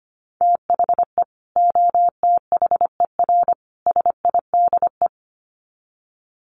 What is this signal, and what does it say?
Morse code, heard as a keyed tone, about 25 wpm, 700 Hz. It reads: THE OTHER SIDE
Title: Sets of 3 Words 25wpm